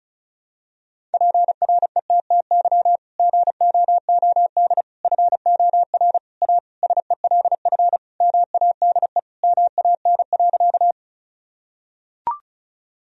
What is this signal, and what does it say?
Morse code, heard as a keyed tone, about 34 wpm, 700 Hz. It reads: PRETTY GOOD FOR A SELF MADE MAN.
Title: Sentences from Top 1000 Words - Review Part 1 - 35wpm